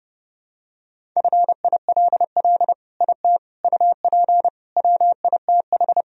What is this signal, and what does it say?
Morse code, heard as a keyed tone, about 32 wpm, 700 Hz. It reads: FILL IT UP WITH
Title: Sentences from Top 1000 Words - Review Part 3 - 30wpm